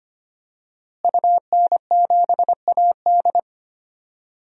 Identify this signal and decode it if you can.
Morse code, heard as a keyed tone, about 25 wpm, 700 Hz. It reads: UN7AD
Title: DXCC-100 Call Signs - 5 Characters - Repeated 3x - 25wpm